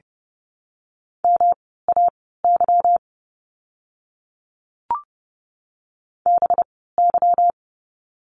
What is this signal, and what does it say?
Morse code, heard as a keyed tone, about 30 wpm, spaced as for 17 wpm, 700 Hz. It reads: MAY BY
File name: course / 30:10wpm / 230 Lesson 37f - Letter x - Words 30wpm